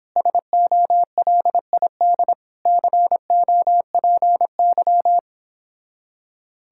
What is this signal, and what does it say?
Morse code, heard as a keyed tone, about 31 wpm, 700 Hz. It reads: SOLID COPY